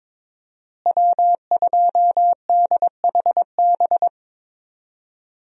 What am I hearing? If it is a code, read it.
Morse code, heard as a keyed tone, about 22 wpm, 700 Hz. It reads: W2DHB
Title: USA Call Signs - 5 Characters - Repeated 3x - 22wpm